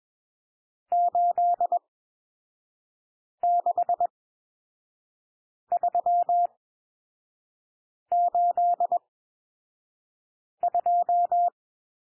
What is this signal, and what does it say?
Morse code, heard as a keyed tone, about 21 wpm, spaced as for 5 wpm, 700 Hz. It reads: 86382
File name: cwmp3 / N1 SOUND NUMBERS 21 wpm